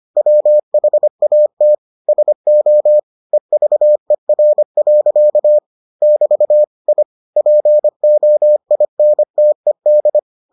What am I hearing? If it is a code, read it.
Morse code, heard as a keyed tone, about 25 wpm, 600 Hz. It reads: WHAT SO EVER. <BT> I POINTED